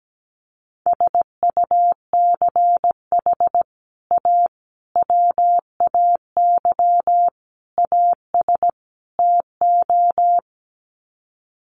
Morse code, heard as a keyed tone, about 17 wpm, 700 Hz. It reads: SUCH A WAY AS TO
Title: Sets of 5 Words Encore2 17wpm